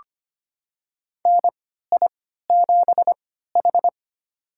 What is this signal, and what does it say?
Morse code, heard as a keyed tone, about 25 wpm, spaced as for 14 wpm, 700 Hz. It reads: NI7H
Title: USA Call Signs - 3x Character Spacing - 25wpm